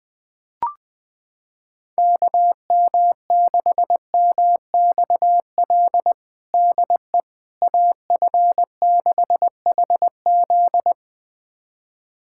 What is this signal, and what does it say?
Morse code, heard as a keyed tone, about 20 wpm, 700 Hz. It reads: KM6MXL DE AF6HZ
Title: USA Callsign DE Callsign 20wpm